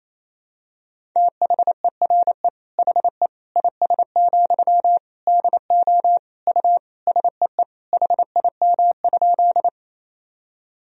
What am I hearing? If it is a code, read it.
Morse code, heard as a keyed tone, about 28 wpm, 700 Hz. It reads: THERE HE IS, DO U SEE HIM?
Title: Sentences from Top 100 Words and Abbreviations 28wpm